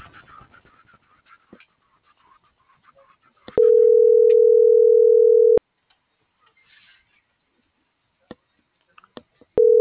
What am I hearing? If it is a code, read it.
Ringback tone (North America), 440 + 480 Hz.
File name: CENTRAL TERRESTRE / out erick.sanjuan